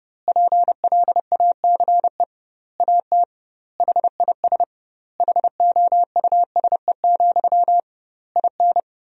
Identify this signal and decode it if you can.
Morse code, heard as a keyed tone, about 32 wpm, 700 Hz. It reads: PLACE AT HIS HOUSE, IN
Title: 2x Word-Spacing - Sentences from Top 300 Words - 30wpm